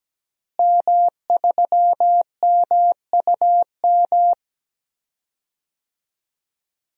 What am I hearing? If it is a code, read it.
Morse code, heard as a keyed tone, about 17 wpm, 700 Hz. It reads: M3MUM
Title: DXCC-100 Call Signs - 5 Characters - Repeated 2x - 17wpm